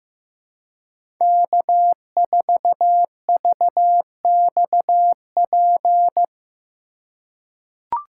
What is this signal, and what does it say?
Morse code, heard as a keyed tone, about 16 wpm, 700 Hz. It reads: K4VXP
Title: USA Call Signs - 3x Character Spacing - 15wpm